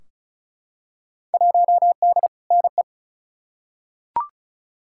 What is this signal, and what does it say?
Morse code, heard as a keyed tone, about 36 wpm, 700 Hz. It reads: 1D NE
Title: Field Day Exchanges - 35wpm